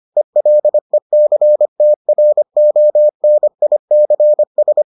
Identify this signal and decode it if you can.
Morse code, heard as a keyed tone, about 25 wpm, 600 Hz. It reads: ELECTRONICS